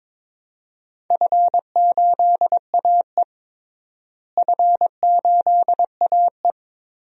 Morse code, heard as a keyed tone, about 22 wpm, 700 Hz. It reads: F8AE F8AE
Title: DXCC-100 Call Signs - 4 Characters - Repeated 3x - 22wpm